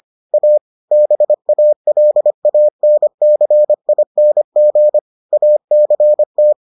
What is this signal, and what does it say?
Morse code, heard as a keyed tone, about 25 wpm, 600 Hz. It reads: A BALANCING ACT